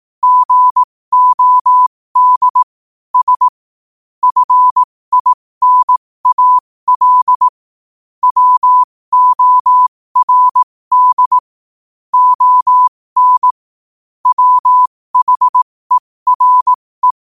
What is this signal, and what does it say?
Morse code, heard as a keyed tone, about 18 wpm, spaced as for 15 wpm, 1000 Hz. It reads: GOD S FINAL WORD ON WHERE
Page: New quotes every day in morse code at 15 Words per minute.